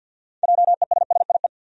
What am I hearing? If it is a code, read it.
Morse code, heard as a keyed tone, about 49 wpm, 700 Hz. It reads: JESSIE
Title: USA Common Names - 50wpm